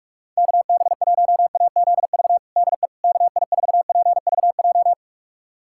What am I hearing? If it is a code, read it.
Morse code, heard as a keyed tone, about 46 wpm, 700 Hz. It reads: KB1AZV DE KI4PVJ
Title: USA Callsign DE Callsign - 6 Characters - 45wpm